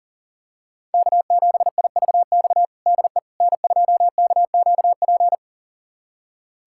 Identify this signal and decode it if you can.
Morse code, heard as a keyed tone, about 37 wpm, 700 Hz. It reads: K7IVX DE N2KQP
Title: USA Callsign DE Callsign - 5 Characters - 40wpm